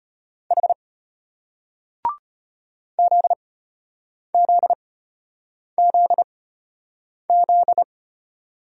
Morse code, heard as a keyed tone, about 33 wpm, spaced as for 8 wpm, 700 Hz. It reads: H ZZZZ